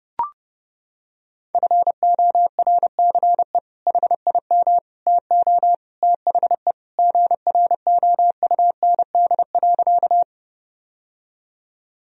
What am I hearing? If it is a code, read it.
Morse code, heard as a keyed tone, about 32 wpm, 700 Hz. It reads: FORCE HIM TO THE GROUND.
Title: Sentences from Top 1000 Words - Review Part 3 - 30wpm